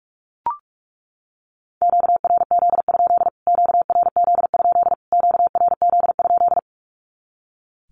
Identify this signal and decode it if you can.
Morse code, heard as a keyed tone, about 44 wpm, 700 Hz. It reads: QRZ? QRZ? QRZ?
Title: Q-Codes Repeated 3x 45wpm